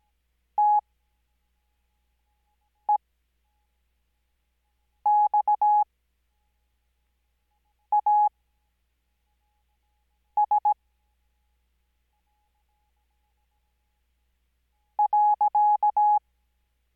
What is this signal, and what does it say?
Morse code, heard as a keyed tone, about 17 wpm, spaced as for 4 wpm, 840 Hz. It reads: TEXAS .
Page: Farnsworth Spacing: 15WPM Character speed Spaced at 5WPM.